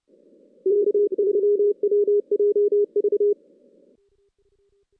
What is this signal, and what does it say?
Morse code, heard as a keyed tone, about 29 wpm, 410 Hz. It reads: K3WJV
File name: 2016 JAN CW NAQP / 2016 Jan naqp k5ot - 0-6